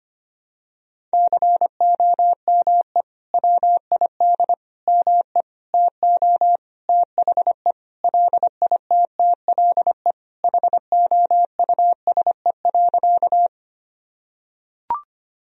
Morse code, heard as a keyed tone, about 25 wpm, 700 Hz. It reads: COME WID ME TO THE LITTLE HOUSE.
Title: Sentences from Top 200 Words and Abbreviations 25wpm